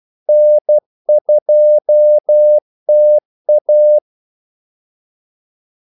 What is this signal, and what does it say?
Morse code, heard as a keyed tone, about 12 wpm, 600 Hz. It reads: N2TA
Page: 2 ekstra ordmellemrum mellem kaldesignalerne